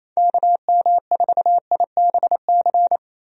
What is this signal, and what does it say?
Morse code, heard as a keyed tone, about 28 wpm, 700 Hz. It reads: KM4IBC
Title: USA Call Signs - 6 Characters - 28wpm